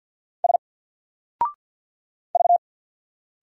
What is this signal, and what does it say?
Morse code, heard as a keyed tone, about 53 wpm, 700 Hz. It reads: S V
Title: Single Letter-Number 50wpm